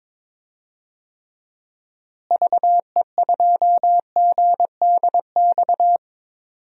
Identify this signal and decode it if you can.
Morse code, heard as a keyed tone, about 22 wpm, 700 Hz. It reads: VE2GDX
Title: DXCC-100 Call Signs - 6 Characters - Repeated 3x - 22wpm